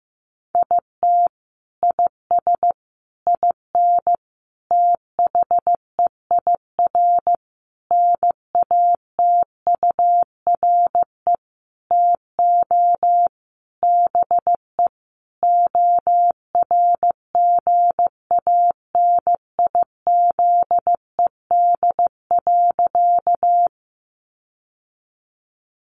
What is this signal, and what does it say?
Morse code, heard as a keyed tone, about 15 wpm, 700 Hz. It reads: IT IS IN THEIR NATURE TO BE ORGANIZED.
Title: Sentences from Top 1300 Words 15wpm